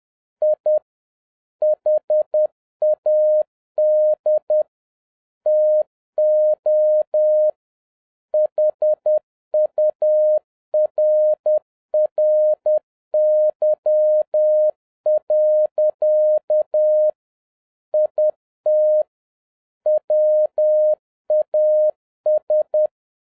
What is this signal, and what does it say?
Morse code, heard as a keyed tone, about 10 wpm, 610 Hz. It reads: I HAD TO HURRY. IT WAS